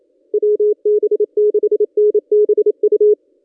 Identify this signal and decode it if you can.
Morse code, heard as a keyed tone, about 27 wpm, 420 Hz. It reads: WB6NBU
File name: contests / ss-cw-k5tr-op-n5rz -1600